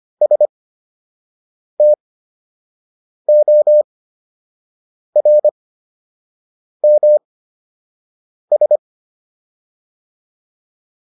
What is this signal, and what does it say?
Morse code, heard as a keyed tone, about 25 wpm, spaced as for 6 wpm, 600 Hz. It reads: STORMS